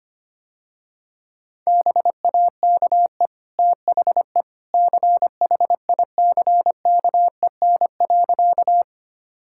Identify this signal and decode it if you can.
Morse code, heard as a keyed tone, about 25 wpm, 700 Hz. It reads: BAKE THE CHICKEN.
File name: Easy Sentences with Hints - 25wpm